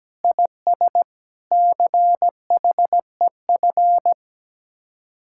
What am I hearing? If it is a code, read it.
Morse code, heard as a keyed tone, about 17 wpm, 700 Hz. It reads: IS CHEF
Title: Speed-Racing Common QSO Phrases - 17wpm